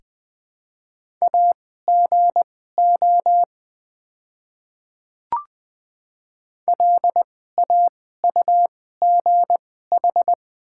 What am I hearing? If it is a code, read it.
Morse code, heard as a keyed tone, about 20 wpm, spaced as for 14 wpm, 700 Hz. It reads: AGO LAUGH
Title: 2x Character Spacing - Top 500 Words - 20wpm